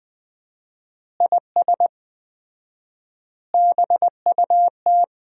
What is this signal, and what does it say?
Morse code, heard as a keyed tone, about 21 wpm, 700 Hz. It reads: IS BUT
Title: 4x Word-Spacing - Sentences from Top 100 Words - 20wpm